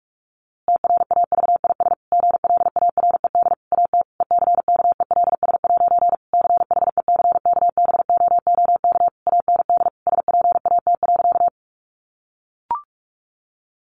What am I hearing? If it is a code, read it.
Morse code, heard as a keyed tone, about 46 wpm, 700 Hz. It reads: TRAVIS GLARED AT ECKELS' CHECKBOOK AND SPAT.
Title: A Sound Of Thunder 45wpm